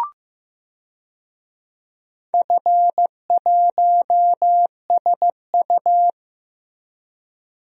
Morse code, heard as a keyed tone, about 15 wpm, 700 Hz. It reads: F1SU
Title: DXCC-100 Call Signs - 4 Characters - Repeated 2x - 15wpm